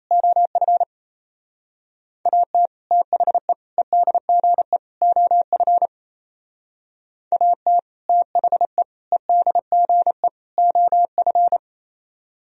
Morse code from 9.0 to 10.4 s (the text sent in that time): EDGE